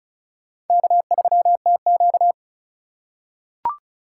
Morse code, heard as a keyed tone, about 34 wpm, 700 Hz. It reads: K3TQ
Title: USA Call Signs - 4 Characters - 4x Character Spacing - 35wpm